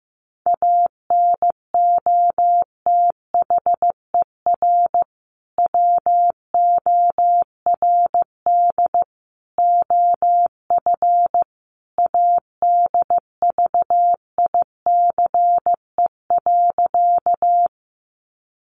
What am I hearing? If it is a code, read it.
Morse code, heard as a keyed tone, about 15 wpm, 700 Hz. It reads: ANOTHER WORD OF ADVICE.
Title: Sentences from Top 1100 Words 15wpm